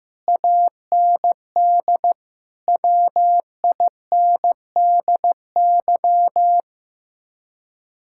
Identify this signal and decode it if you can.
Morse code, heard as a keyed tone, about 15 wpm, 700 Hz. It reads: AND WINDY
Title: Speed-Racing Common QSO Phrases Encore - 15wpm